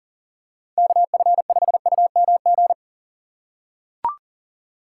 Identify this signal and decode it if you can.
Morse code, heard as a keyed tone, about 40 wpm, 700 Hz. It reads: KF5UMG
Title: USA Call Signs - 6 Characters - 4x Character Spacing - 40wpm